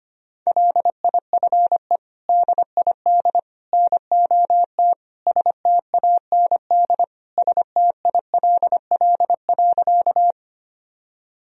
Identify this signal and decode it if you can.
Morse code, heard as a keyed tone, about 25 wpm, 700 Hz. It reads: LIFE DID NOT STAND STILL.
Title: Sentences from Top 1000 Words - Review Part 1 - 25wpm